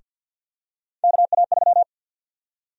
Morse code, heard as a keyed tone, about 50 wpm, 700 Hz. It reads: KA3
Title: USA Call Sign Prefixes - 50wpm